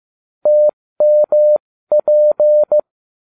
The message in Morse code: TMP

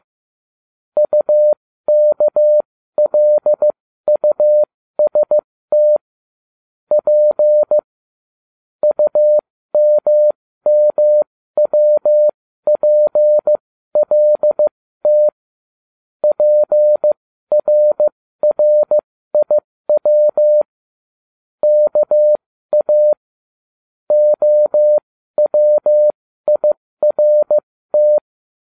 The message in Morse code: UKLUST P UMMWPLT PRRIW KA OWIRT

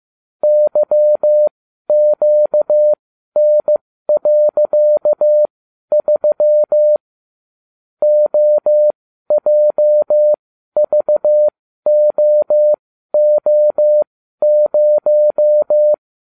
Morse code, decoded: YQN.3 OJVOO0

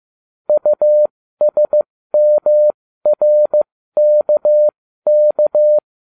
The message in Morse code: USMRKK

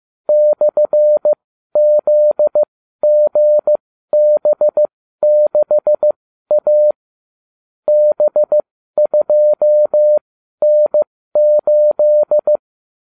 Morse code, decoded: /ZGB6A B2N8

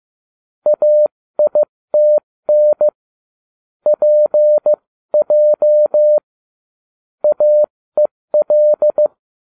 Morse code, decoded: AITN PJ AEL

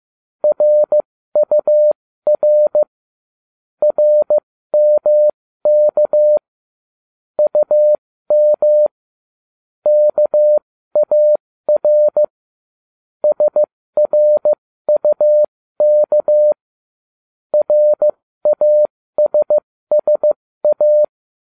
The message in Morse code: RUR RMK UM KAR SRUK RASSA

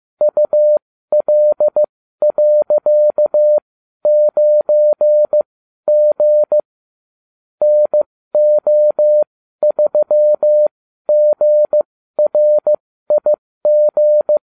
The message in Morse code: UL.9G NO3GRIG

